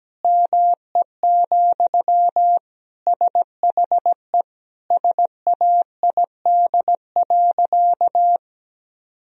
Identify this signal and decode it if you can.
Morse code, heard as a keyed tone, about 17 wpm, 700 Hz. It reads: ME, SHE SAID.
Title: Sentences from Top 1000 Words - Review Part 1 - 17wpm